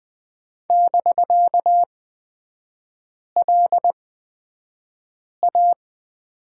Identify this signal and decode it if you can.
Morse code, heard as a keyed tone, about 20 wpm, spaced as for 5 wpm, 700 Hz. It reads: <BK>LA
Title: 250 Lesson 40e - BK - Three Characters 20wpm